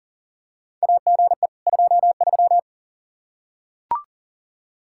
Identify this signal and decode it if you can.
Morse code, heard as a keyed tone, about 40 wpm, 700 Hz. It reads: AGE 23
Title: Practice QSO - Exchange 3 - More Info - 40wpm